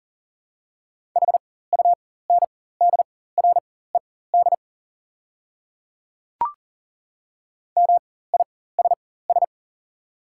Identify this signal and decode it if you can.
Morse code, heard as a keyed tone, about 40 wpm, spaced as for 19 wpm, 700 Hz. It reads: HUNDRED MISS
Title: 4x Character Spacing - Top 500 Words - 40wpm